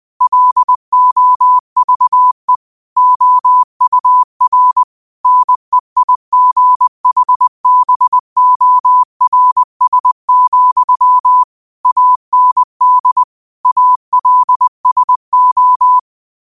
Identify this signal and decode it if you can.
Morse code, heard as a keyed tone, about 20 wpm, 1000 Hz. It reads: LOVE OUR NEIGHBORS, AND ALSO
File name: qotdfiles / 20wpm